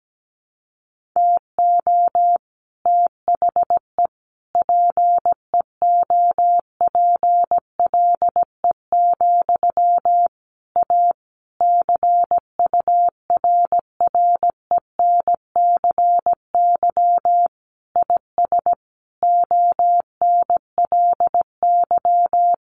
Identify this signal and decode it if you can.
Morse code, heard as a keyed tone, about 17 wpm, 700 Hz. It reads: TO THE PEOPLE, A CURRENCY IS ONLY
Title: Sentences from Top 1500 Words 17wpm